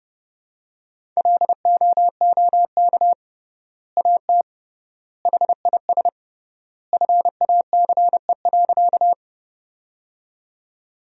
Morse code, heard as a keyed tone, about 32 wpm, 700 Hz. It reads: LOOK AT HIS FACE.
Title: 3x Word-Spacing - Sentences from Top 300 Words - 30wpm